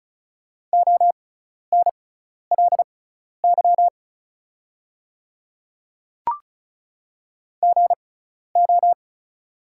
Morse code, heard as a keyed tone, about 36 wpm, spaced as for 12 wpm, 690 Hz. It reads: ONLY GO
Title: 6x Character Spacing - Top 200 Words - 35wpm